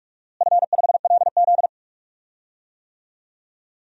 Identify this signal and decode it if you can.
Morse code, heard as a keyed tone, about 46 wpm, 700 Hz. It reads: F5LZ
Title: DXCC-100 Call Signs - 4 Characters - Repeated 2x - 45wpm